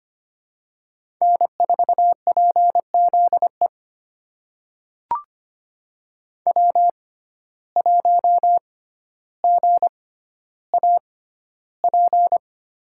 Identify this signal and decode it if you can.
Morse code, heard as a keyed tone, about 25 wpm, spaced as for 13 wpm, 700 Hz. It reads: N4PZE W1GAP
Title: USA Call Signs - 5 Characters - 6x Character Spacing - 25wpm